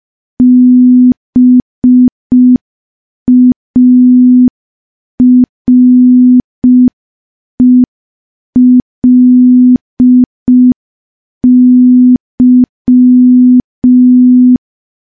Morse code, heard as a keyed tone, about 5 wpm, 250 Hz. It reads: BARELY
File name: morse / morse-training-05WPM